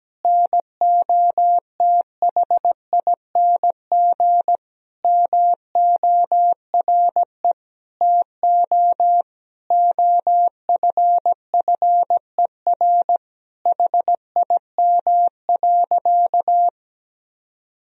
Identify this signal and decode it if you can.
Morse code, heard as a keyed tone, about 17 wpm, 700 Hz. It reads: NOTHING MORE TO OFFER HIM.
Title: Sentences from Top 1000 Words - Review Part 4 - 17wpm